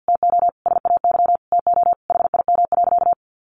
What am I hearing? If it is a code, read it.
Morse code, heard as a keyed tone, about 50 wpm, 700 Hz. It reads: TO SAY TO HIM.
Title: Sentences from Top 1000 Words 50wpm